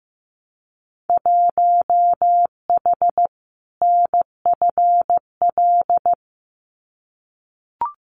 Morse code, heard as a keyed tone, about 15 wpm, 700 Hz. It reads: 1H NFL
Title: Winter Field Day Exchanges 15wpm